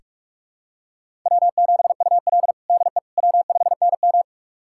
Morse code, heard as a keyed tone, about 42 wpm, 690 Hz. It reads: W7UL DE W5NM
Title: USA Callsign DE Callsign - 4 Characters - 45wpm